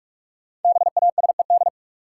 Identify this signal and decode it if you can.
Morse code, heard as a keyed tone, about 47 wpm, 700 Hz. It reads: BASED
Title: Sets of 2 Words - New Word Drill - 45wpm